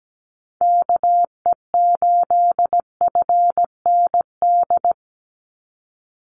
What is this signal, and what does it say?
Morse code, heard as a keyed tone, about 17 wpm, 700 Hz. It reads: KE8FND
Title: USA Call Signs 17wpm